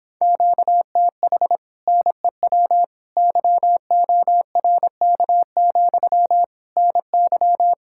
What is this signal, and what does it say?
Morse code, heard as a keyed tone, about 26 wpm, 700 Hz. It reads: QTH NEW YORK, NY
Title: Speed-Racing Common QSO Phrases - 22wpm